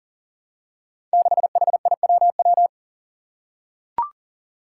Morse code, heard as a keyed tone, about 40 wpm, 700 Hz. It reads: 6HIWW
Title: Five Challenging Letter-Number Combinations 40wpm